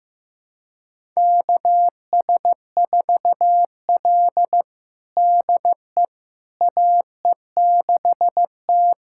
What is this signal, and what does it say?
Morse code, heard as a keyed tone, about 15 wpm, 700 Hz. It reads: KS4L DE AE6T
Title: USA Callsign DE Callsign - 4 Characters - 15wpm